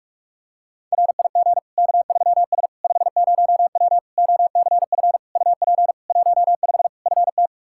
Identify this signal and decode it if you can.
Morse code, heard as a keyed tone, about 46 wpm, 700 Hz. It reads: RIG K3S 50W OCF UP 15 FT
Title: K1USN SST QSO Practice 45wpm